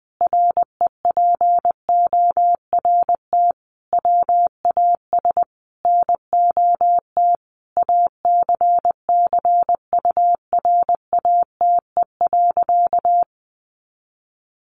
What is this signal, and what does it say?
Morse code, heard as a keyed tone, about 20 wpm, 700 Hz. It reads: REPORT WAS NOT ACCURATE.
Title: Sentences from Top 1200 Words 20wpm